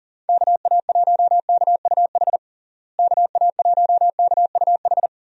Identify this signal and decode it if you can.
Morse code, heard as a keyed tone, about 40 wpm, 700 Hz. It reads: KA1KUH KA1KUH
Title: USA Call Signs - 6 Characters - Repeated 3x - 40wpm